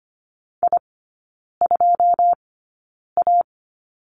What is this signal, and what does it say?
Morse code, heard as a keyed tone, about 25 wpm, spaced as for 9 wpm, 690 Hz. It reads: I2A
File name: Lesson 14e - Number 2 - Three Characters 25wpm